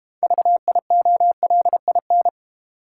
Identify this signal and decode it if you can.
Morse code, heard as a keyed tone, about 32 wpm, 700 Hz. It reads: VIOLIN